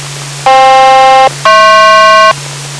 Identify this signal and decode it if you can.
Aircraft SELCAL, heard as two pairs of tones. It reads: FK-HP